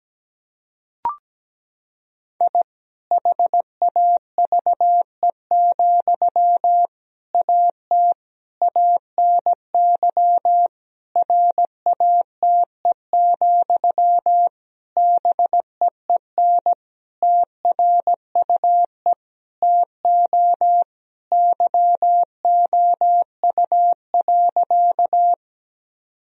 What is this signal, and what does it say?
Morse code, heard as a keyed tone, about 17 wpm, 700 Hz. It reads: I HAVE, AT ANY RATE, BEEN TRUE TO YOU.
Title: Sentences from Top 1000 Words - Review Part 4 - 17wpm